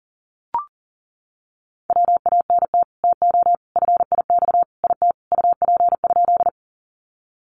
Morse code, heard as a keyed tone, about 40 wpm, 700 Hz. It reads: WANT TO FIX IT UP?
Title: Sentences from Top 1400 Words 40wpm